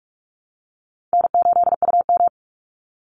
Morse code, heard as a keyed tone, about 44 wpm, 700 Hz. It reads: N8UM
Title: Call Signs Repeated 2x - 45wpm